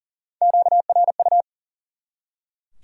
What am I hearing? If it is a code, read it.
Morse code, heard as a keyed tone, about 40 wpm, 700 Hz. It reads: QRU
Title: Top 90 QSO Elements 40wpm